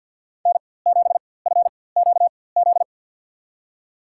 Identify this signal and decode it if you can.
Morse code, heard as a keyed tone, about 54 wpm, spaced as for 24 wpm, 700 Hz. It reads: N7FQZ
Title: USA Call Signs - 4x Character Spacing - 50wpm